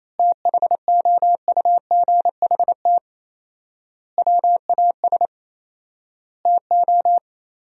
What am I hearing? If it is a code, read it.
Morse code, heard as a keyed tone, about 28 wpm, 700 Hz. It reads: THOUGHT WAS TO
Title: 4x Word-Spacing - Sentences from Top 400 Words - 28wpm